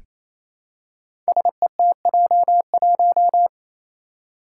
Morse code, heard as a keyed tone, about 27 wpm, 700 Hz. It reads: SETJ1
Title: Five Challenging Letter-Number Combinations 28wpm